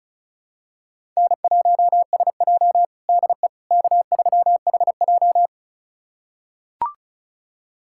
Morse code, heard as a keyed tone, about 36 wpm, 700 Hz. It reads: N1SJ DE K3HJ
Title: USA Callsign DE Callsign - 4 Characters - 35wpm